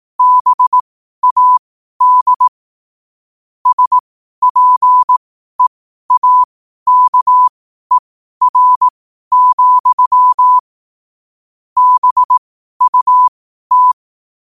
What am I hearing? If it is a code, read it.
Morse code, heard as a keyed tone, about 18 wpm, spaced as for 12 wpm, 1000 Hz. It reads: BAD SPEAKER, BUT
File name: qotdfiles / 12wpm